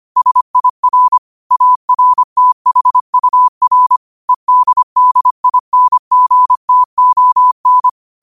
Morse code, heard as a keyed tone, about 25 wpm, 1000 Hz. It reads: SIR ARTHUR EDDINGTON
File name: qotdfiles / 25wpm